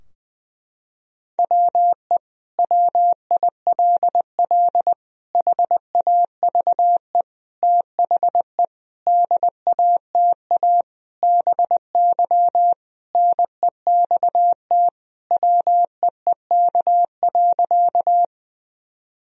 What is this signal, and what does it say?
Morse code, heard as a keyed tone, about 21 wpm, 700 Hz. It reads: WE WILL HAVE THE DATA BY NEXT WEEK.
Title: Sentences from Top 1000 Words - Review Part 3 - 20wpm